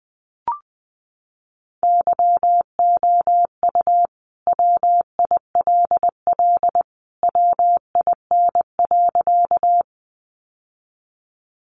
Morse code, heard as a keyed tone, about 20 wpm, 700 Hz. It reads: YOU WILL WIN.
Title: Sentences from Top 1500 Words 20wpm